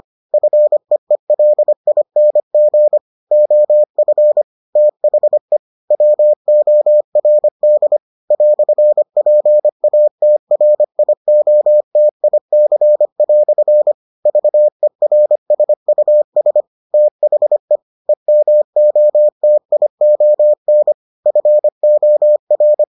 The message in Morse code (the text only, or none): FEELING OF THE WORD "PATRIOTIC" VERSUS THE EMOTION FOR